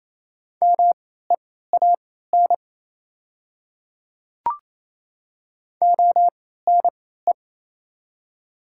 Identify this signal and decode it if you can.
Morse code, heard as a keyed tone, about 28 wpm, spaced as for 16 wpm, 700 Hz. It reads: MEAN ONE
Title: 3x Character Spacing - Top 500 Words - 28wpm